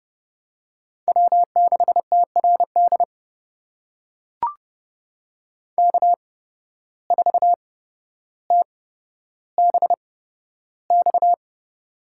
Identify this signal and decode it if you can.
Morse code, heard as a keyed tone, about 30 wpm, spaced as for 13 wpm, 700 Hz. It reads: W6TRD K4TBX
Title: USA Call Signs - 5 Characters - 8x Character Spacing - 30wpm